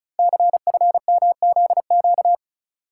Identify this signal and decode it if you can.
Morse code, heard as a keyed tone, about 36 wpm, 690 Hz. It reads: CFMZQ
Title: Code Groups - 5 Letters - 35wpm